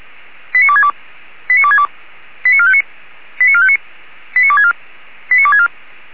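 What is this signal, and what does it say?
Five-tone selective call (ZVEI), heard as quick runs of five tones, tones of about 70 ms. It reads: 89262 89262 89459 89459 89264 89264
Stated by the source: ZVEI five-tone call received by SDR.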